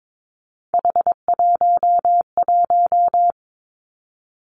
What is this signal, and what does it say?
Morse code, heard as a keyed tone, about 22 wpm, 700 Hz. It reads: H11
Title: Three Challenging Letter-Number Combinations 22wpm